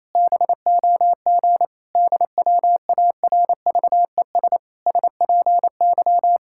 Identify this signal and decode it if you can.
Morse code, heard as a keyed tone, about 28 wpm, 700 Hz. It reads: BOG DWARVES SPY
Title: Pangram Challenge v2 - 28wpm